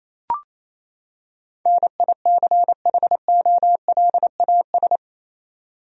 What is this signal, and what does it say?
Morse code, heard as a keyed tone, about 28 wpm, 700 Hz. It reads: NICHOLAS
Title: USA Common Names - 28wpm